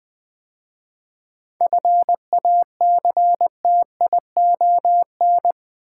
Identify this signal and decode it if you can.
Morse code, heard as a keyed tone, about 20 wpm, 700 Hz. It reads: FACTION
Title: 7 Letter Words - First 3 Letters Given - 20wpm